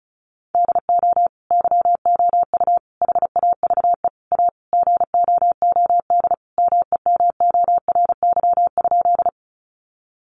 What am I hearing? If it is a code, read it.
Morse code, heard as a keyed tone, about 34 wpm, 700 Hz. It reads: DO YOU HAVE A GOOD MEMORY?
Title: Sentences from Top 1300 Words 35wpm